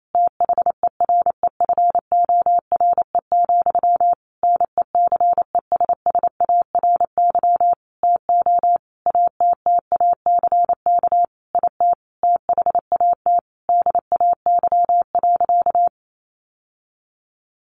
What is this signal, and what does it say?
Morse code, heard as a keyed tone, about 28 wpm, 700 Hz. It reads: THEREFORE, NECESSARY TO ATTACK IT THAT DAY.
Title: Sentences from Top 1500 Words 28wpm